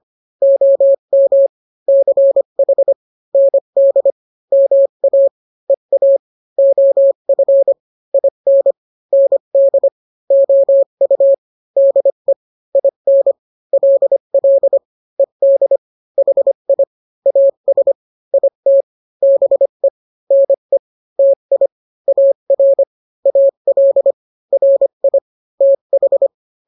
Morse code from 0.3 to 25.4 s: OM CH ND MA EA OF IN ND OU DE IN LL ED HI AS IT BE NE TI AR AL RI